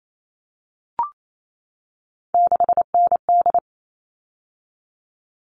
Morse code, heard as a keyed tone, about 28 wpm, 700 Hz. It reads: 6ND
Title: All Cut Numbers - 3-Digit 28wpm